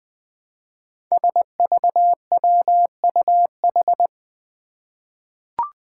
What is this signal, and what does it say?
Morse code, heard as a keyed tone, about 20 wpm, 700 Hz. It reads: SVWUH
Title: Five Challenging Letter-Number Combinations 20wpm